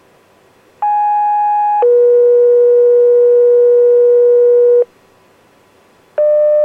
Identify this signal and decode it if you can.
Two-tone page, heard as a one-second tone, then a three-second tone, about 830 then 480 Hz. It is heard from 0.8 to 4.9 s.